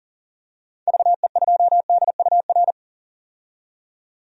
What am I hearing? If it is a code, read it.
Morse code, heard as a keyed tone, about 40 wpm, 700 Hz. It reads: VE2DUR
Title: DXCC-100 Call Signs - 6 Characters - 40wpm